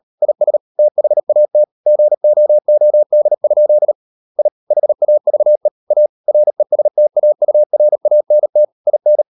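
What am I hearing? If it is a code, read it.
Morse code, heard as a keyed tone, about 38 wpm, 600 Hz. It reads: IS THAT GOOD? I HAVE A RESTAURANT IN
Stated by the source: Morse code practice at 38 WPM, sending Ken Follett's novel Never in English, from CW med Gnister.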